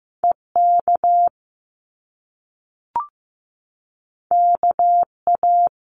Morse code, heard as a keyed tone, about 15 wpm, 700 Hz. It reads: EK KA
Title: Top Two Letter Combinations in Words 15wpm